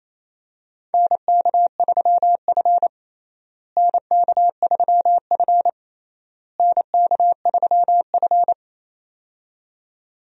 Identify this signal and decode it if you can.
Morse code, heard as a keyed tone, about 28 wpm, 700 Hz. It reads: NK3F NK3F NK3F
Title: USA Call Signs - 4 Characters - Repeated 3x - 28wpm